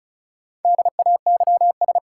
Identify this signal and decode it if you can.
Morse code, heard as a keyed tone, about 36 wpm, 700 Hz. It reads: DAYS
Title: Sentences from Top 1000 Words - Review Part 5 - New Word Drill - 35wpm